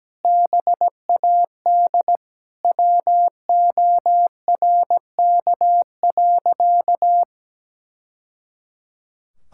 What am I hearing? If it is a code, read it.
Morse code, heard as a keyed tone, about 17 wpm, 700 Hz. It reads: BAD WORK.
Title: Sentences from Top 1000 Words - Review Part 6 - 17wpm